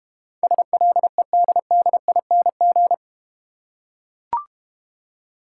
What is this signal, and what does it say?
Morse code, heard as a keyed tone, about 32 wpm, 700 Hz. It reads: SLEDDING